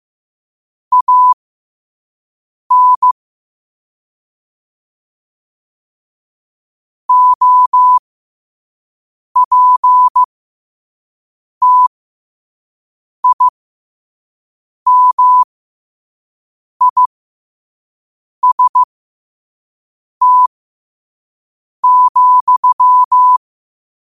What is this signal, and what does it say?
Morse code, heard as a keyed tone, about 15 wpm, spaced as for 5 wpm, 1000 Hz. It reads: AN OPTIMIST,
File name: qotdfiles / 5wpm